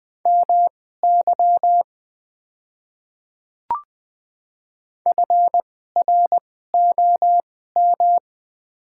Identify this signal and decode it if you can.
Morse code, heard as a keyed tone, about 20 wpm, spaced as for 14 wpm, 700 Hz. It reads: MY FROM
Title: 2x Character Spacing - Top 100 Words - 20wpm